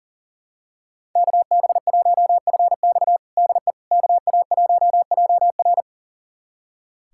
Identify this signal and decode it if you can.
Morse code, heard as a keyed tone, about 40 wpm, 700 Hz. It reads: KB1FX DE KA1JR
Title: USA Callsign DE Callsign - 5 Characters - 40wpm